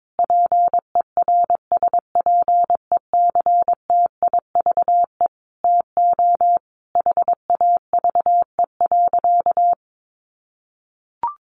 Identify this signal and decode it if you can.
Morse code, heard as a keyed tone, about 22 wpm, 700 Hz. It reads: PERSPECTIVE TO HAVE.
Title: Sentences from Top 1200 Words 22wpm